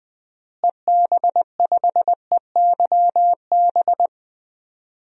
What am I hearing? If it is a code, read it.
Morse code, heard as a keyed tone, about 20 wpm, 700 Hz. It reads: EB5EYB
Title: DXCC-100 Call Signs - 6 Characters - Repeated 2x - 20wpm